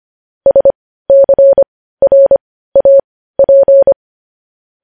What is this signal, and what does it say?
Morse code, heard as a keyed tone, about 25 wpm, spaced as for 15 wpm, 550 Hz. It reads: SCRAP
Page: news headlines in morse code at 15 wpm